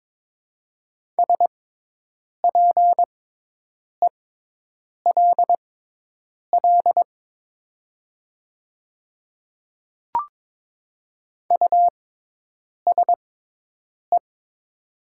Morse code, heard as a keyed tone, about 22 wpm, spaced as for 8 wpm, 700 Hz. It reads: SPELL USE
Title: 6x Character Spacing - Top 300 Words - 22wpm